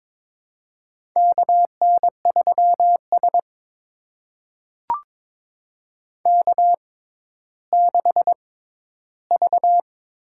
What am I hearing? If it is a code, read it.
Morse code, heard as a keyed tone, about 22 wpm, spaced as for 12 wpm, 700 Hz. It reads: KN3S K6V
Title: USA Call Signs - 4 Characters - 6x Character Spacing - 22wpm